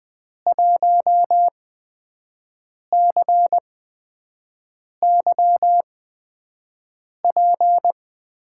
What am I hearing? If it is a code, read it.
Morse code, heard as a keyed tone, about 20 wpm, spaced as for 5 wpm, 700 Hz. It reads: 1CYP
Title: USA Call Signs - 8x Character Spacing - 20wpm